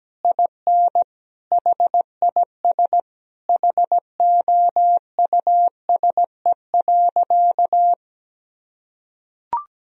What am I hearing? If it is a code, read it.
Morse code, heard as a keyed tone, about 17 wpm, 700 Hz. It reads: IN HIS HOUSE.
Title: Sentences from Top 1000 Words - Review Part 2 - 17wpm